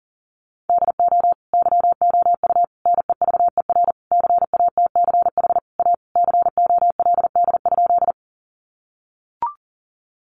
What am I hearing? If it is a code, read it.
Morse code, heard as a keyed tone, about 40 wpm, 700 Hz. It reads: DO YOU NEVER CATCH A COLD?
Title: Sentences from Top 1500 Words 40wpm